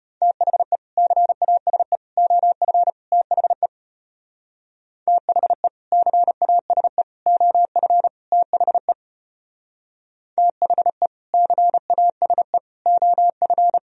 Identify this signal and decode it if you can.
Morse code, heard as a keyed tone, about 30 wpm, 700 Hz. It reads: THE CASE OF THE THE CASE OF THE THE CASE OF
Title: Speed-Racing - Sets of 4 Words 25wpm